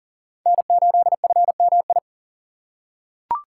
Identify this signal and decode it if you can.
Morse code, heard as a keyed tone, about 40 wpm, 700 Hz. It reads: N8FMI
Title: USA Call Signs - 5 Characters - 3x Character Spacing - 40wpm